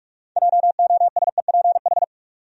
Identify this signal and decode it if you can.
Morse code, heard as a keyed tone, about 46 wpm, 700 Hz. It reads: JOSEPH
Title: USA Common Names - 45wpm